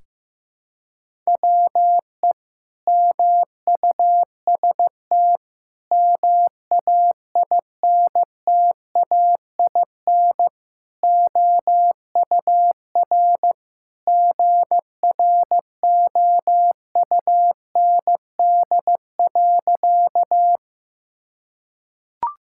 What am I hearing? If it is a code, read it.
Morse code, heard as a keyed tone, about 15 wpm, 700 Hz. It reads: WE MUST MAINTAIN OUR GROUND.
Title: Sentences from Top 1000 Words - Review Part 5 - 15wpm